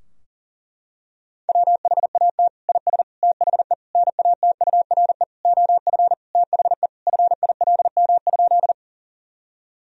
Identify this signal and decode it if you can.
Morse code, heard as a keyed tone, about 40 wpm, 700 Hz. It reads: WHAT IS THE NATURE OF THE FILM?
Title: Sentences from Top 1000 Words - Review Part 3 - 40wpm